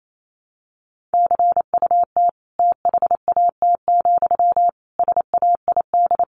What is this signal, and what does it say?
Morse code, heard as a keyed tone, about 28 wpm, 700 Hz. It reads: CUT THAT, SAID
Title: A Sound Of Thunder 28wpm